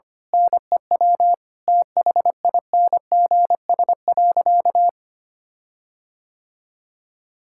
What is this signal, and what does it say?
Morse code, heard as a keyed tone, about 25 wpm, 700 Hz. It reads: NEW THINGS.
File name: practice / 25wpm / Difficult Sentences with Hints - 25wpm